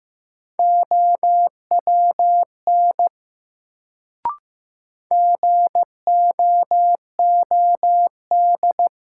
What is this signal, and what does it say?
Morse code, heard as a keyed tone, about 15 wpm, 700 Hz. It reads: OWN GOOD
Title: Sets of 2 Words - New Word Drill - 15wpm